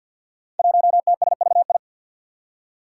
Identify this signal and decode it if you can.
Morse code, heard as a keyed tone, about 53 wpm, 700 Hz. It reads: 1TSVI
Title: Five Challenging Letter-Number Combinations 50wpm